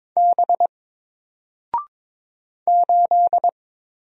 Morse code, heard as a keyed tone, about 22 wpm, 700 Hz. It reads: B 8
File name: Single-Letter-Number-22wpm